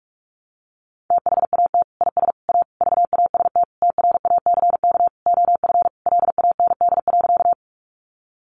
Morse code, heard as a keyed tone, about 44 wpm, 700 Hz. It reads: THAT IS A VAST TRACK OF LAND.
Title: Sentences from Top 1300 Words 45wpm